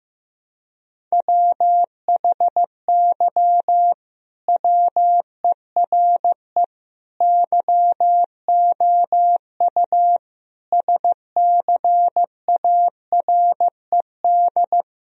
Morse code, heard as a keyed tone, about 15 wpm, 700 Hz. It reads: WHY WERE YOU SCARED
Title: Sentences from Top 1000 Words - Review Part 4 - 15wpm